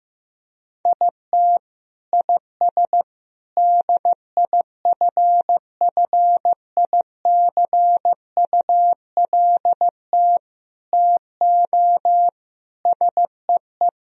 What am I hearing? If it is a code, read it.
Morse code, heard as a keyed tone, about 15 wpm, 700 Hz. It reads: IT IS DIFFICULT TO SEE
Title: Sentences from Top 1000 Words - Review Part 4 - 15wpm